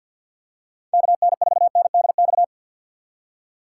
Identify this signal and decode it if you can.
Morse code, heard as a keyed tone, about 47 wpm, 700 Hz. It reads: KN4NDX
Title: USA Call Signs - 6 Characters - 6x Character Spacing - 50wpm